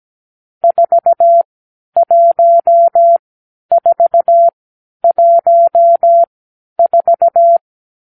Morse code, heard as a keyed tone, about 17 wpm, spaced as for 10 wpm, 680 Hz. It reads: 41414
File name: k7qo / Lesson 46 (1-4)